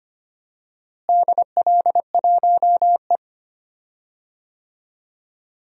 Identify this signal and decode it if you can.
Morse code, heard as a keyed tone, about 25 wpm, 700 Hz. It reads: DL1E
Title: DXCC-100 Call Signs - 4 Characters - Repeated 2x - 25wpm